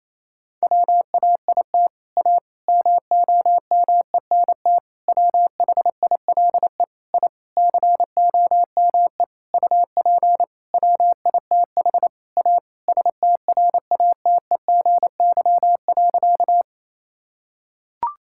Morse code, heard as a keyed tone, about 28 wpm, 700 Hz. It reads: WAIT A MOMENT WHILE I COME UP WITH A STRATEGY.
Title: Sentences from Top 1000 Words - Review Part 5 - 28wpm